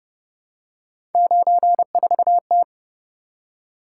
Morse code, heard as a keyed tone, about 29 wpm, 700 Hz. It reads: 94T